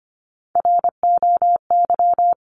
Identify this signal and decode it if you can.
Morse code, heard as a keyed tone, about 25 wpm, 700 Hz. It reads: ROY
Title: CWT Exchange 25wpm